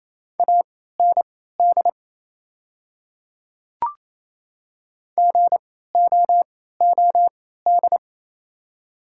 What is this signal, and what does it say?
Morse code, heard as a keyed tone, about 27 wpm, spaced as for 16 wpm, 700 Hz. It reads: AND GOOD
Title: 3x Character Spacing - Top 300 Words - 28wpm